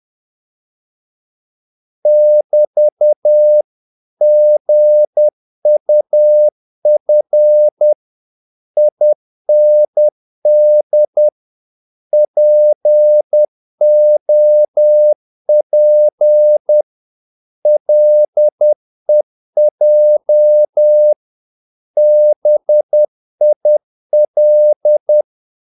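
Morse code, heard as a keyed tone, about 10 wpm, 600 Hz. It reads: <BT> GUF IND POP LEJ BIL
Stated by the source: Morse practice of short Danish words at 10 wpm, from CW med Gnister.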